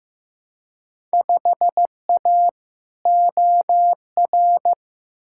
Morse code, heard as a keyed tone, about 15 wpm, 700 Hz. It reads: 5A OR
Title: Field Day Exchanges - 15wpm